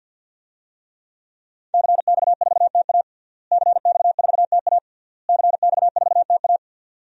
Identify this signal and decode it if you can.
Morse code, heard as a keyed tone, about 50 wpm, 700 Hz. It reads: CX4TA CX4TA CX4TA
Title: DXCC-100 Call Signs - 5 Characters - Repeated 3x - 50wpm